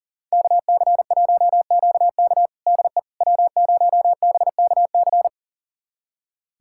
Morse code, heard as a keyed tone, about 37 wpm, 690 Hz. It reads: KC1QK DE W0BKC
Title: USA Callsign DE Callsign - 5 Characters - 40wpm